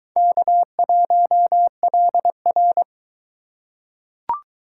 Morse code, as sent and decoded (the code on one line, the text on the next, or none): -.- .---- .-.. .-.
K1LR